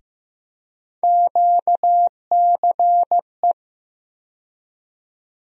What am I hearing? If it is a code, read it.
Morse code, heard as a keyed tone, about 15 wpm, 700 Hz. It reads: QCE
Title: USA Call Sign Postfixes - 3 Letters 15wpm